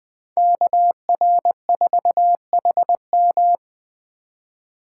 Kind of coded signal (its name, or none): Morse code